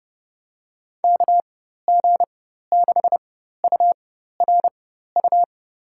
Morse code, heard as a keyed tone, about 30 wpm, spaced as for 14 wpm, 700 Hz. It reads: KG6URU
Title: USA Call Signs - 4x Character Spacing - 30wpm